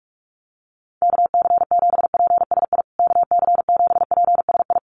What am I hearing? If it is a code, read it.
Morse code, heard as a keyed tone, about 44 wpm, 700 Hz. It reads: KC7PSI KC7PSI
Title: USA Call Signs Repeated 3x 45wpm